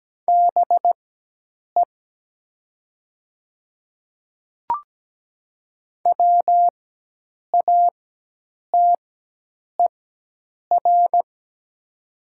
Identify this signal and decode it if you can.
Morse code, heard as a keyed tone, about 17 wpm, spaced as for 8 wpm, 700 Hz. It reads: BE WATER
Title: 4x Character Spacing - Top 200 Words - 17wpm